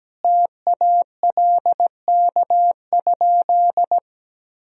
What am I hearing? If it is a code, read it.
Morse code, heard as a keyed tone, about 17 wpm, 700 Hz. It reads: TALK?
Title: Sentences from Top 1000 Words - Review Part 1 - 17wpm